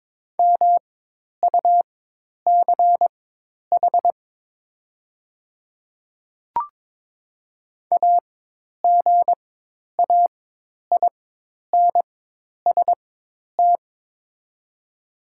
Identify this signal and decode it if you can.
Morse code, heard as a keyed tone, about 22 wpm, spaced as for 10 wpm, 690 Hz. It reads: MUCH AGAINST
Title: 4x Character Spacing - Top 500 Words - 22wpm